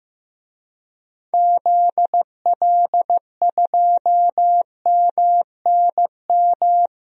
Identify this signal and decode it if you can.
Morse code, heard as a keyed tone, about 15 wpm, 700 Hz. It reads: ZL2MNM
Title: DXCC-100 Call Signs - 6 Characters - 15wpm